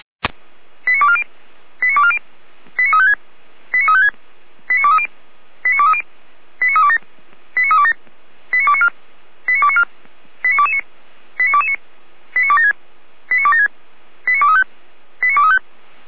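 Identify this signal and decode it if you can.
Five-tone selective call (ZVEI), heard as quick runs of five tones, tones of about 70 ms. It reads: 89240 89240 89356 89356 89230 89230 89327 89327 89294 89294 89229 89229 89276 89276 89235 89235
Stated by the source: ZVEI five-tone selective call 89208, received with an SDR.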